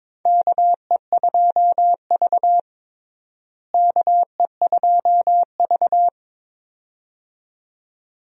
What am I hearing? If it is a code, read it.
Morse code, heard as a keyed tone, about 22 wpm, 700 Hz. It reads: KE2V KE2V
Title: USA Call Signs - 4 Characters - Repeated 2x - 22wpm